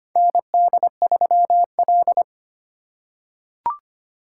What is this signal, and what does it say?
Morse code, heard as a keyed tone, about 25 wpm, 700 Hz. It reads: ND3L
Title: K1USN SST QSO Practice - 25wpm